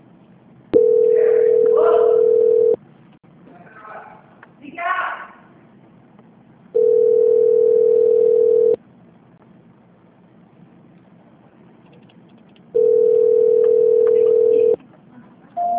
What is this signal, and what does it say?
Ringback tone (North America), 440 + 480 Hz.
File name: out- -2039- - - .4393